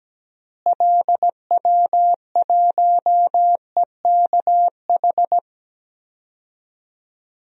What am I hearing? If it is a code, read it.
Morse code, heard as a keyed tone, about 17 wpm, 700 Hz. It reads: LW1EKH
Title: DXCC-100 Call Signs - 6 Characters - Repeated 2x - 17wpm